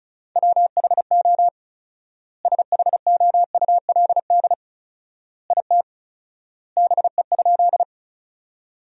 Morse code, heard as a keyed tone, about 36 wpm, 700 Hz. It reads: WHO SHOULD IT BE?
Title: 4x Word-Spacing - Sentences from Top 300 Words - 35wpm